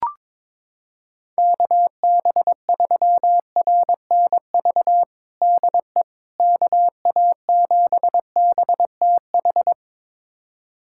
Morse code, heard as a keyed tone, about 21 wpm, 700 Hz. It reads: KB3RNV DE KA7BTH
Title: USA Callsign DE Callsign 22wpm